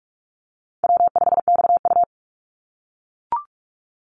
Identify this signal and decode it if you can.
Morse code, heard as a keyed tone, about 44 wpm, 700 Hz. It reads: W5XU
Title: Easy Call Signs - 45wpm